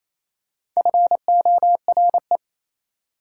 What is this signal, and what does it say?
Morse code, heard as a keyed tone, about 27 wpm, 700 Hz. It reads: FORE
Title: Prefixes and Suffixes 28wpm